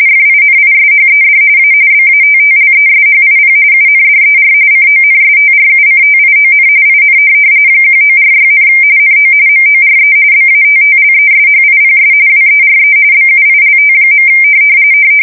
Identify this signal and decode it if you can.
RTTY, about 45.45 baud. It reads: FINISHED A BATTLESHIP UNDER CONSTRUCTION IN 1946, NO MORE BATTLESHIPS WERE BUILT.
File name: fsk170 / USS-New-Jersey